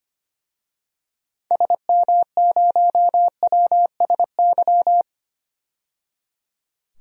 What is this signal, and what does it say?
Morse code, heard as a keyed tone, about 25 wpm, 700 Hz. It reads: SM0WSY
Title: DXCC-100 Call Signs - 6 Characters - 25wpm